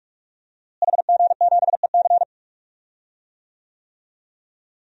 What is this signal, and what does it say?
Morse code, heard as a keyed tone, about 46 wpm, 700 Hz. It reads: HG7EC
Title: DXCC-100 Call Signs - 5 Characters - Repeated 3x - 45wpm